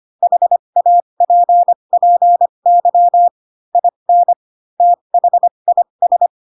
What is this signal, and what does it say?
Morse code, heard as a keyed tone, about 25 wpm, spaced as for 22 wpm, 700 Hz. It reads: HAPPY IN THIS
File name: Pinocchio / Pinocchio ch4 18 25 ik1hge